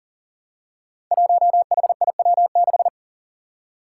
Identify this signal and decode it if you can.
Morse code, heard as a keyed tone, about 40 wpm, 700 Hz. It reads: 1HIW6
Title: Five Challenging Letter-Number Combinations 40wpm